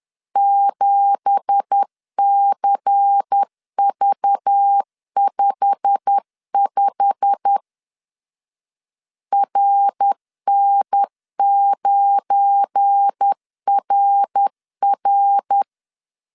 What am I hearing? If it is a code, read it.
Morse code, heard as a keyed tone, about 10 wpm, 800 Hz. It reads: 7CV55 RN9RR
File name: CW / DET / 1 9 1500C S C 10wpm 4557